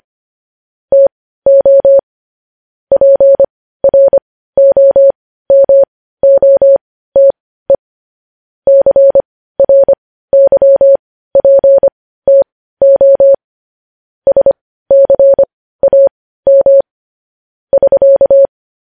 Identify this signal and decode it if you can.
Morse code, heard as a keyed tone, about 25 wpm, spaced as for 15 wpm, 550 Hz. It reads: TO PROMOTE CRYPTO SCAM <SK>
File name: podcast / News headlines-1x-25x15